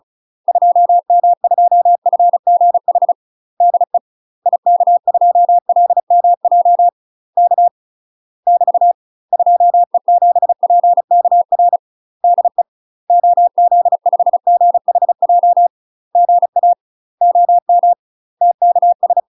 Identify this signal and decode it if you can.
Morse code, heard as a keyed tone, about 34 wpm, 700 Hz. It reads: 2M2FGH DE IK2LMJ K <BT> 2E7PKR DE OZ5GHJ GA OM TKS
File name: UK MASTER QSOs TRAINING / 35WPM / UK MASTER QSOs 0041